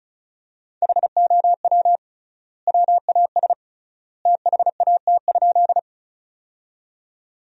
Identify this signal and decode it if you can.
Morse code, heard as a keyed tone, about 36 wpm, 700 Hz. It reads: HOW WAS THAT?
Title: 3x Word-Spacing - Sentences from Top 100 Words - 35wpm